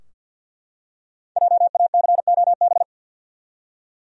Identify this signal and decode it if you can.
Morse code, heard as a keyed tone, about 52 wpm, 700 Hz. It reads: JACOB
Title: USA Common Names - Max 5 Letters - 50wpm